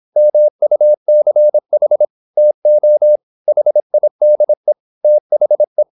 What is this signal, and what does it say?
Morse code, heard as a keyed tone, about 26 wpm, 600 Hz. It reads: MUCH TO HIDE THE
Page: Never 26wpm | CW med Gnister